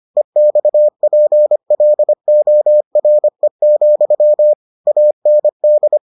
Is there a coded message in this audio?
Yes: Morse code from 0.2 to 6.0 s, heard as a keyed tone, about 25 wpm, 600 Hz. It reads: EXPLORE, AND